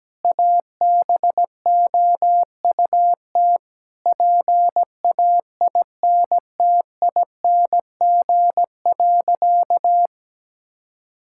Morse code, heard as a keyed tone, about 17 wpm, 700 Hz. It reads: ABOUT PAINTING.
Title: Sentences from Top 1000 Words - Review Part 6 - 17wpm